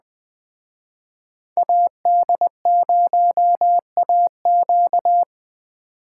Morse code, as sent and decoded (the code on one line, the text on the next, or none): .- -.. ----- .- --.-
AD0AQ